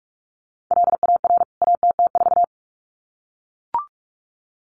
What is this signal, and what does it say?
Morse code, heard as a keyed tone, about 44 wpm, 700 Hz. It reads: LAR ATT4
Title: CWT Exchange 45wpm